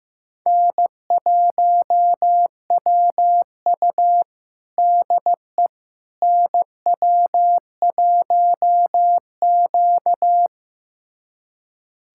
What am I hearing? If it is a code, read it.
Morse code, heard as a keyed tone, about 15 wpm, 700 Hz. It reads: N1WU DE NW1Q
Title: USA Callsign DE Callsign - 4 Characters - 15wpm